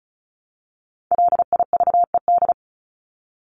Morse code, heard as a keyed tone, about 34 wpm, 700 Hz. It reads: LIVED
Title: ED Suffix - Top 100 Words with Base Words 35wpm